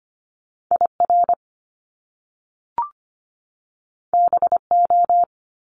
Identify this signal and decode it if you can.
Morse code, heard as a keyed tone, about 25 wpm, 700 Hz. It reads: IR BO
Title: Top Two Letter Combinations in Words 25wpm